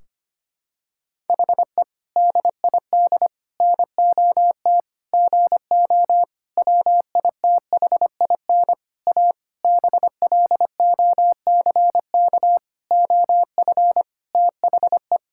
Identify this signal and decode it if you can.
Morse code, heard as a keyed tone, about 25 wpm, 700 Hz. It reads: HE DID NOT GO WITHIN A BLOCK OF THE
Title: Sentences from Top 1600 Words 25wpm